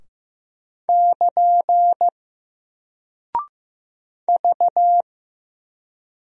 Morse code, heard as a keyed tone, about 15 wpm, 700 Hz. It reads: <KN> V
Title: Single Character (Letter:Number:Punctuation:Prosign) Alt 15wpm